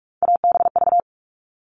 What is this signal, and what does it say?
Morse code, heard as a keyed tone, about 44 wpm, 700 Hz. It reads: ABV
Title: All Cut Numbers - 3-Digit 45wpm